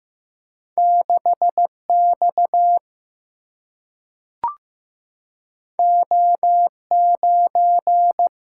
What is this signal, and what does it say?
Morse code, heard as a keyed tone, about 15 wpm, 700 Hz. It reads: 6 X O 9
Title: Single-Letter Single-Number 15wpm